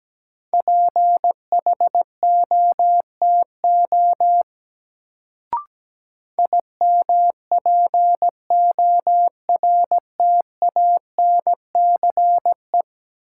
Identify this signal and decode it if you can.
Morse code, heard as a keyed tone, about 17 wpm, 700 Hz. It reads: PHOTO IMPORTANCE
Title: Sentences from Top 1200 Words - New Word Drill - 17wpm